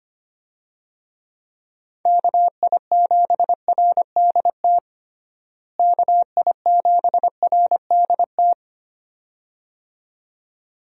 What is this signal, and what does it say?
Morse code, heard as a keyed tone, about 25 wpm, 700 Hz. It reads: KI7RDT KI7RDT
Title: USA Call Signs - 6 Characters - Repeated 2x - 25wpm